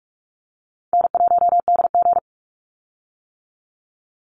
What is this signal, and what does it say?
Morse code, heard as a keyed tone, about 46 wpm, 700 Hz. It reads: N1DG
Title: Easy Call Signs - 45wpm